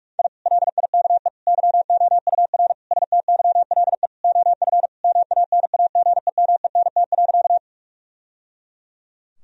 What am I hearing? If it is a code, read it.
Morse code, heard as a keyed tone, about 41 wpm, 700 Hz. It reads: I LIKE YOUR STYLE OF MANAGEMENT.
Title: Sentences from Top 1000 Words - Review Part 4 - 45wpm